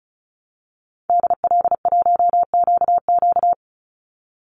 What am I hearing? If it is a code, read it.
Morse code, heard as a keyed tone, about 34 wpm, 700 Hz. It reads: DL1QQ
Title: Call Signs Repeated 2x - 35wpm